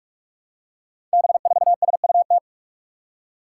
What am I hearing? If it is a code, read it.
Morse code, heard as a keyed tone, about 44 wpm, 700 Hz. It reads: B4SUT
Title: Five Challenging Letter-Number Combinations 45wpm